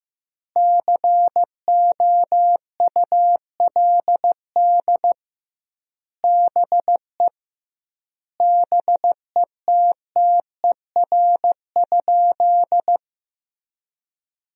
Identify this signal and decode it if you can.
Morse code, heard as a keyed tone, about 15 wpm, 700 Hz. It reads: COULD BE BETTER?
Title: 2x Word-Spacing - Sentences from Top 400 Words - 15wpm